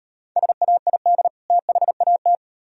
Morse code, heard as a keyed tone, about 38 wpm, 700 Hz. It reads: SAID THAT